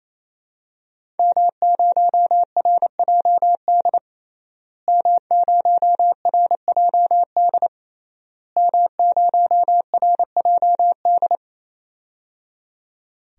Morse code, heard as a keyed tone, about 28 wpm, 700 Hz. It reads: M0RJD M0RJD M0RJD
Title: DXCC-100 Call Signs - 5 Characters - Repeated 3x - 28wpm